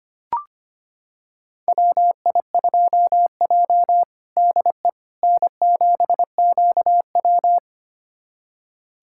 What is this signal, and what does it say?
Morse code, heard as a keyed tone, about 25 wpm, 700 Hz. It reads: WI2J DE N7QW
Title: USA Callsign DE Callsign - 4 Characters - 25wpm